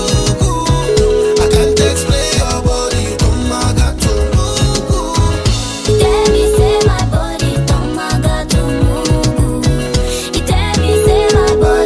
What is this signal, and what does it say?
Ringback tone (Europe, 425 Hz), with noise.